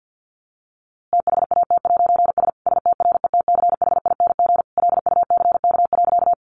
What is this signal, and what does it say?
Morse code, heard as a keyed tone, about 50 wpm, 700 Hz. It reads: THAT'S STRETCHING LUCK.
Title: A Sound Of Thunder 50wpm